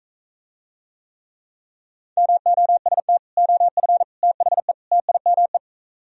Morse code, heard as a keyed tone, about 43 wpm, 700 Hz. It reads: MOST OF THE TIME